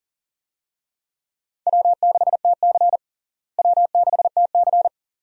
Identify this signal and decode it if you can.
Morse code, heard as a keyed tone, about 40 wpm, 700 Hz. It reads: W6TC W6TC
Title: USA Call Signs - 4 Characters - Repeated 2x - 40wpm